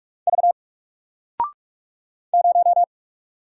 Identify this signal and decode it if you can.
Morse code, heard as a keyed tone, about 44 wpm, 700 Hz. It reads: V 0
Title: Single Letter-Number 45wpm